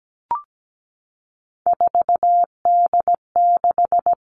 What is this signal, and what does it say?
Morse code, heard as a keyed tone, about 17 wpm, 700 Hz. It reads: 4D6
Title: All Cut Numbers - 3-Digit 17wpm